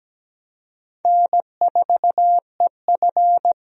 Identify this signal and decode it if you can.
Morse code, heard as a keyed tone, about 17 wpm, 700 Hz. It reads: N4EF
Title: USA Call Signs - 4 Characters - 17wpm